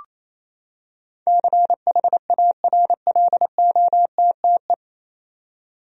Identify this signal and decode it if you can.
Morse code, heard as a keyed tone, about 28 wpm, 700 Hz. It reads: CHARLOTTE
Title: USA Common Names - 28wpm